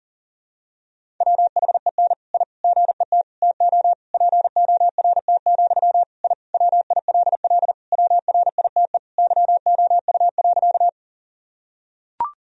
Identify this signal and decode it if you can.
Morse code, heard as a keyed tone, about 40 wpm, 700 Hz. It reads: WHEN I GET TO PORT, I WILL WRITE YOU.
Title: Sentences from Top 1000 Words - Review Part 1 - 40wpm